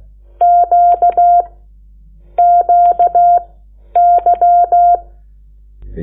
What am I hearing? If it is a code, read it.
Morse code, heard as a keyed tone, about 16 wpm, spaced as for 10 wpm, 700 Hz. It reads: Q QY
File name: Lecciones / CLASE Nº8